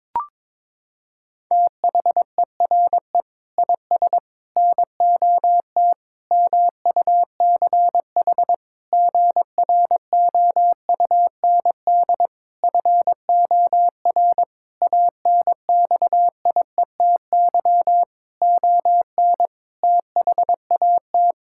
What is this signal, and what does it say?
Morse code, heard as a keyed tone, about 22 wpm, 700 Hz. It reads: THERE IS NOT MUCH GROUND FOR ANXIETY ON THAT
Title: Sentences from Top 1600 Words 22wpm